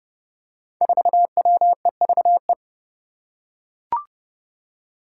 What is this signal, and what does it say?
Morse code, heard as a keyed tone, about 29 wpm, 700 Hz. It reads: 4WEVE
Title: Five Challenging Letter-Number Combinations 30wpm